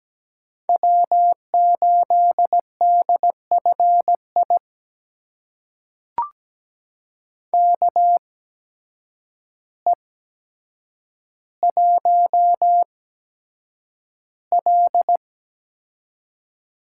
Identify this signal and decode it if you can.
Morse code, heard as a keyed tone, about 17 wpm, spaced as for 8 wpm, 700 Hz. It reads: W8DFI KE1L
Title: USA Call Signs - 5 Characters - 8x Character Spacing - 17wpm